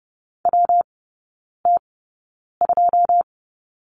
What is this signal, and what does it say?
Morse code, heard as a keyed tone, about 30 wpm, spaced as for 9 wpm, 700 Hz. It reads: WT2
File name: Lesson 19e - Letter w - Three Characters 30wpm